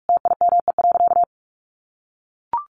Morse code, heard as a keyed tone, about 44 wpm, 700 Hz. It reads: TIME.
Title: Sentences from Top 1300 Words 45wpm